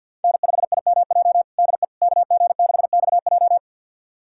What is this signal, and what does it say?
Morse code, heard as a keyed tone, about 49 wpm, 700 Hz. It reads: N5IMJ DE KG6XJ
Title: USA Callsign DE Callsign - 5 Characters - 50wpm